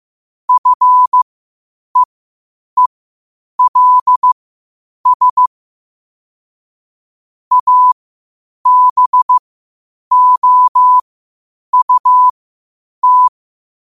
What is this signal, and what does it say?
Morse code, heard as a keyed tone, about 15 wpm, spaced as for 8 wpm, 1000 Hz. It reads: FEELS ABOUT
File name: qotdfiles / 8wpm